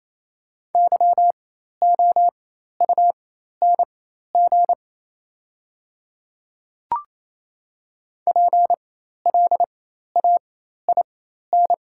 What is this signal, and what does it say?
Morse code, heard as a keyed tone, about 28 wpm, spaced as for 13 wpm, 700 Hz. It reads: YOUNG PLAIN
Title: 4x Character Spacing - Top 500 Words - 28wpm